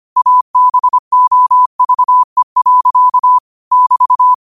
Morse code, heard as a keyed tone, about 25 wpm, 1000 Hz. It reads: ABOVE. <BT>
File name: qotdfiles / 25wpm